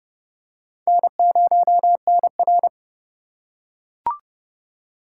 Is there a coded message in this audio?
Yes: Morse code from 0.9 to 2.7 s, heard as a keyed tone, about 30 wpm, 700 Hz. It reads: N0NR